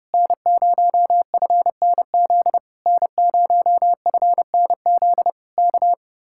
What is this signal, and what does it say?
Morse code, heard as a keyed tone, about 29 wpm, 700 Hz. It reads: N0FNZ N0FNZ K